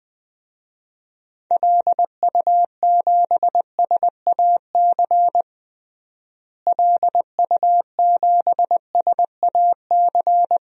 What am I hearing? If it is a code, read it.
Morse code, heard as a keyed tone, about 20 wpm, 700 Hz. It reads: LU7SAC LU7SAC
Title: DXCC-100 Call Signs - 6 Characters - Repeated 3x - 20wpm